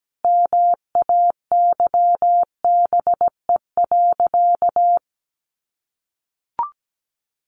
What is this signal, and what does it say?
Morse code, heard as a keyed tone, about 17 wpm, 700 Hz. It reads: MAYBE.
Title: Sentences from Top 1000 Words 17wpm